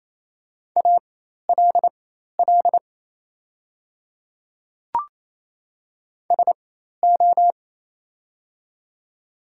Morse code, heard as a keyed tone, about 28 wpm, spaced as for 13 wpm, 690 Hz. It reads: ALL SO
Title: 4x Character Spacing - Top 100 Words - 28wpm